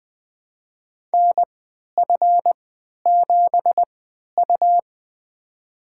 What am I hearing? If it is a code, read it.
Morse code, heard as a keyed tone, about 20 wpm, spaced as for 11 wpm, 700 Hz. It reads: NF7U
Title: USA Call Signs - 3x Character Spacing - 20wpm